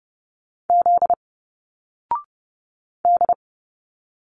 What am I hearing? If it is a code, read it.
Morse code, heard as a keyed tone, about 30 wpm, 700 Hz. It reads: Z D
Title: Single Letters 30wpm